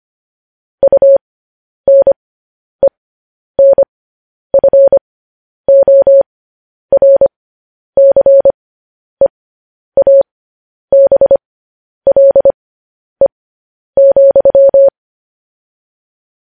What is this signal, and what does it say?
Morse code, heard as a keyed tone, about 25 wpm, spaced as for 10 wpm, 550 Hz. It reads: UNENFORCEABLE,
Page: news headlines in morse code at 10 wpm